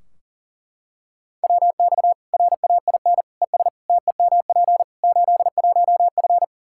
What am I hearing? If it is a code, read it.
Morse code, heard as a keyed tone, about 40 wpm, 700 Hz. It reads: WX RAIN ES TEMP 81F
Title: Practice QSO - Exchange 2 - Info - 40wpm